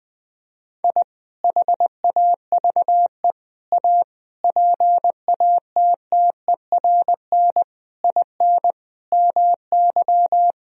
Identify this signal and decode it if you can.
Morse code, heard as a keyed tone, about 20 wpm, 700 Hz. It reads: I HAVE A PATTERN IN MY
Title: Sentences from Top 1000 Words - Review Part 2 - 20wpm